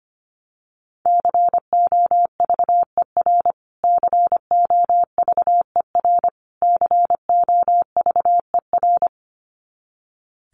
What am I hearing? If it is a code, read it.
Morse code, heard as a keyed tone, about 25 wpm, 700 Hz. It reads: COVER COVER COVER
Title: Top 300 Words Repeated 25wpm